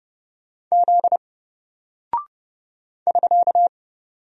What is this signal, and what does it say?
Morse code, heard as a keyed tone, about 30 wpm, 700 Hz. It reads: Z <SK>
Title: Single Character (Letter:Number:Punctuation:Prosign) Alt 30wpm